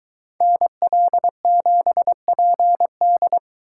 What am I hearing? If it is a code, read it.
Morse code, heard as a keyed tone, about 23 wpm, 700 Hz. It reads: NL7PD